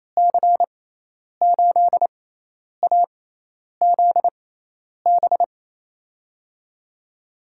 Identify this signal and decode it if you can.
Morse code, heard as a keyed tone, about 27 wpm, spaced as for 10 wpm, 700 Hz. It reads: C8AZB
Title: USA Call Signs - 6x Character Spacing - 28wpm